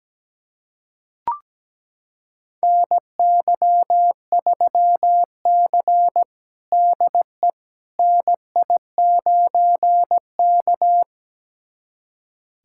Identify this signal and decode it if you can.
Morse code, heard as a keyed tone, about 17 wpm, 700 Hz. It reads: NY3C DE NI9K
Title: USA Callsign DE Callsign - 4 Characters - 17wpm